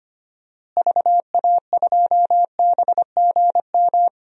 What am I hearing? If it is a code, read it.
Morse code, heard as a keyed tone, about 25 wpm, 700 Hz. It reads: VA2BGM
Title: DXCC-100 Call Signs - 6 Characters - Repeated 2x - 25wpm